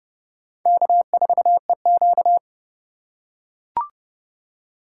Morse code, heard as a keyed tone, about 30 wpm, 700 Hz. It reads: K4EQ